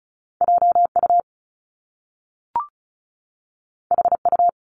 Morse code, heard as a keyed tone, about 36 wpm, 700 Hz. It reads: JU HU
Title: Top Two Letter Combinations in Words 35wpm